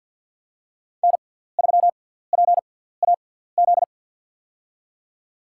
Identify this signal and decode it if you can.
Morse code, heard as a keyed tone, about 51 wpm, spaced as for 17 wpm, 700 Hz. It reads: N3PAZ
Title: USA Call Signs - 5 Characters - 6x Character Spacing - 50wpm